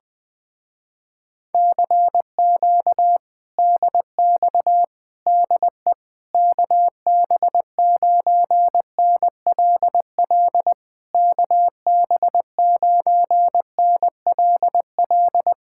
Morse code, heard as a keyed tone, about 20 wpm, 700 Hz. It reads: CQ DX DE KB9NLL KB9NLL
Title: Speed-Racing Common QSO Phrases Encore - 17wpm